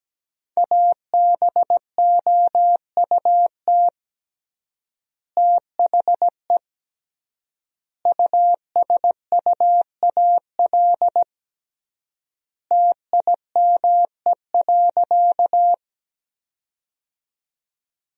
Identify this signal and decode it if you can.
Morse code, heard as a keyed tone, about 17 wpm, 700 Hz. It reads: ABOUT THE USUAL TIME.
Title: 3x Word-Spacing - Sentences from Top 400 Words - 17wpm